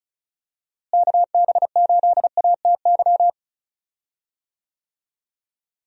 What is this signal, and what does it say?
Morse code, heard as a keyed tone, about 34 wpm, 700 Hz. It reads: KB8ATY
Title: USA Call Signs - 6 Characters - Repeated 2x - 35wpm